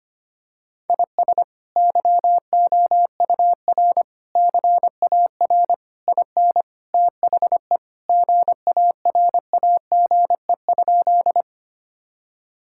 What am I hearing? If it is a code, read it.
Morse code, heard as a keyed tone, about 25 wpm, 700 Hz. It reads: IS YOUR CAR IN THE GARAGE?
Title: Sentences from Top 1600 Words 25wpm